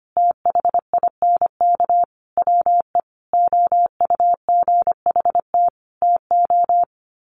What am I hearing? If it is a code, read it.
Morse code, heard as a keyed tone, about 25 wpm, 700 Hz. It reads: THINK WE OUGHT TO
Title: Sets of 5 Words Encore 25wpm